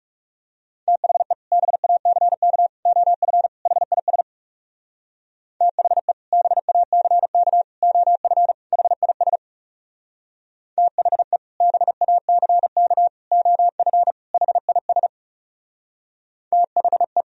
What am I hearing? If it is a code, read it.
Morse code, heard as a keyed tone, about 40 wpm, 700 Hz. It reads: THE BACK OF HIS THE BACK OF HIS THE BACK OF HIS THE